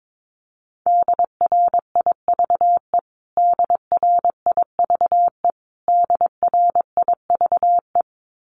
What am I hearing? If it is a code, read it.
Morse code, heard as a keyed tone, about 22 wpm, 700 Hz. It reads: DRIVE DRIVE DRIVE
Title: Top 500 Words Repeated 22wpm